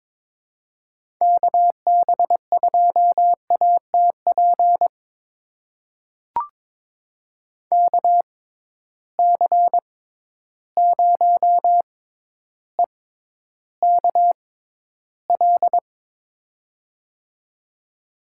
Morse code, heard as a keyed tone, about 22 wpm, spaced as for 11 wpm, 700 Hz. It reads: KB2ATP KC0EKL
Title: USA Call Signs - 6 Characters - 6x Character Spacing - 22wpm